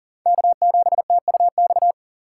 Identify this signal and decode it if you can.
Morse code, heard as a keyed tone, about 40 wpm, 700 Hz. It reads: K7TUX
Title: USA Call Signs - 5 Characters - 6x Character Spacing - 40wpm